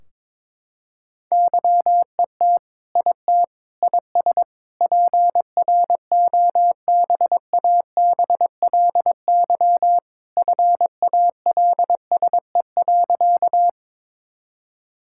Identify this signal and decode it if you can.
Morse code, heard as a keyed tone, about 22 wpm, 700 Hz. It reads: YET IT IS PROBABLY FALSE.
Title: Sentences from Top 1600 Words 22wpm